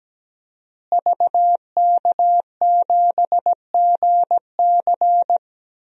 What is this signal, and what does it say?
Morse code, heard as a keyed tone, about 17 wpm, 700 Hz. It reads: VK7GC
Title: DXCC-100 Call Signs - 5 Characters - Repeated 3x - 17wpm